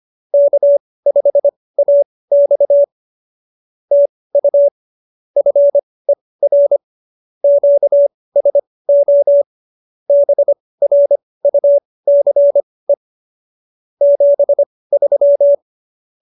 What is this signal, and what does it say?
Morse code, heard as a keyed tone, about 25 wpm, spaced as for 18 wpm, 570 Hz. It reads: K5AX TU FER QSO BRUCE 73
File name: qso201b 18